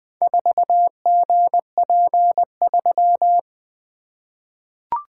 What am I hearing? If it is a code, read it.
Morse code, heard as a keyed tone, about 21 wpm, 700 Hz. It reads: 4GP3
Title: Four Letter-Number Combinations - 20wpm